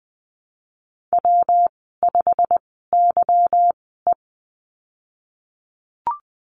Morse code, heard as a keyed tone, about 20 wpm, spaced as for 15 wpm, 700 Hz. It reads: W5YE
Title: 147 Lesson 24g - Letter y - Call Signs 20wpm